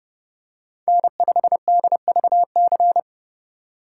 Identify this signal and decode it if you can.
Morse code, heard as a keyed tone, about 31 wpm, 700 Hz. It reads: N5DVC
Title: USA Call Signs - 5 Characters - 4x Character Spacing - 30wpm